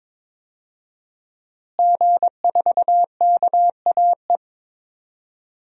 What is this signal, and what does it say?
Morse code, heard as a keyed tone, about 22 wpm, 700 Hz. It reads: G4KAE
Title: DXCC-100 Call Signs - 5 Characters - Repeated 2x - 22wpm